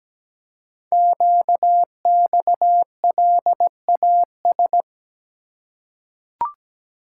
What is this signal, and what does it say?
Morse code, heard as a keyed tone, about 17 wpm, 700 Hz. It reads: QXLAS
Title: Code Groups - 5 Letters - 17wpm